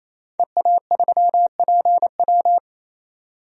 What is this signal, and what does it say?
Morse code, heard as a keyed tone, about 28 wpm, 700 Hz. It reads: EA3PW
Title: DXCC-100 Call Signs - 5 Characters - 28wpm